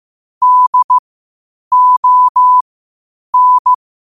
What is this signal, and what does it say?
Morse code, heard as a keyed tone, about 15 wpm, spaced as for 8 wpm, 1000 Hz. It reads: DON
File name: qotdfiles / 8wpm